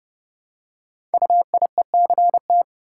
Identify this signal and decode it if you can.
Morse code, heard as a keyed tone, about 30 wpm, 700 Hz. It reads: UIECT
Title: Code Groups - 5 Letters - 30wpm